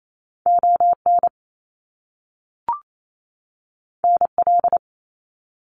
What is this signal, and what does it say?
Morse code, heard as a keyed tone, about 28 wpm, 700 Hz. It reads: ON NL
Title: Canadian Provinces and Territories 28wpm